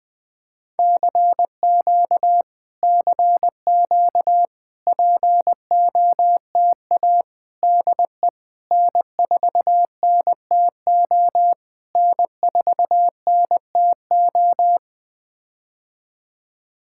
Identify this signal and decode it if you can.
Morse code, heard as a keyed tone, about 20 wpm, 700 Hz. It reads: CQ CQ POTA DE N4NTO N4NTO
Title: POTA QSO Practice - 20wpm